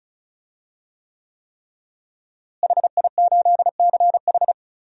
Morse code, heard as a keyed tone, about 36 wpm, 700 Hz. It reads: HI8CH
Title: DXCC-100 Call Signs - 5 Characters - Repeated 3x - 35wpm